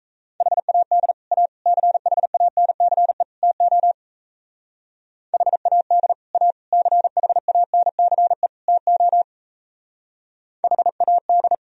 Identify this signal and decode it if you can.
Morse code, heard as a keyed tone, about 35 wpm, 700 Hz. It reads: HAD A CHANCE TO HAD A CHANCE TO HAD